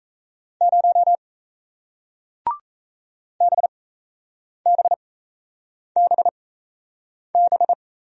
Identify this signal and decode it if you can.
Morse code, heard as a keyed tone, about 38 wpm, spaced as for 8 wpm, 700 Hz. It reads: 0 BBBB